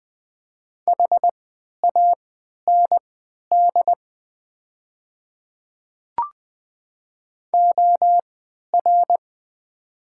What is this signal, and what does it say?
Morse code, heard as a keyed tone, about 20 wpm, spaced as for 11 wpm, 700 Hz. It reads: HAND OR
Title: 3x Character Spacing - Top 200 Words - 20wpm